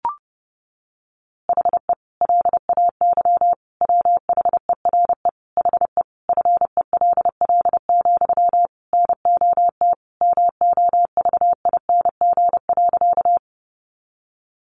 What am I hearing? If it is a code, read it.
Morse code, heard as a keyed tone, about 30 wpm, 700 Hz. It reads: HE LAY WHERE HE FELL, NOT MOVING.
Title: A Sound Of Thunder 30wpm